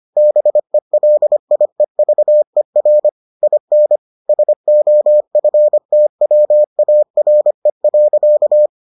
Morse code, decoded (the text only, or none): BELIEVER IN SOFTWARE.